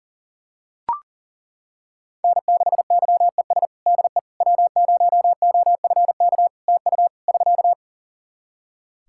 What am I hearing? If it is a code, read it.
Morse code, heard as a keyed tone, about 40 wpm, 700 Hz. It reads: N6YES DE W0OFK TU <SK>
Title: K1USN SST QSO Practice 40wpm